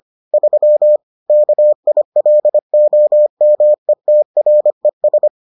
Morse code, heard as a keyed tone, about 25 wpm, 600 Hz. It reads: 3 KILOMETRES